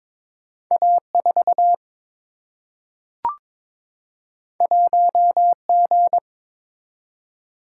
Morse code, heard as a keyed tone, about 22 wpm, 700 Hz. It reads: A 4 1 G
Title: Single-Letter Single-Number 22wpm